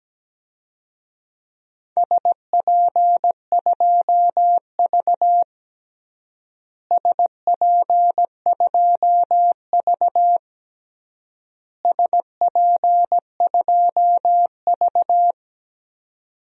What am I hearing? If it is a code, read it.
Morse code, heard as a keyed tone, about 17 wpm, 700 Hz. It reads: SP2V SP2V SP2V
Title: DXCC-100 Call Signs - 4 Characters - Repeated 3x - 17wpm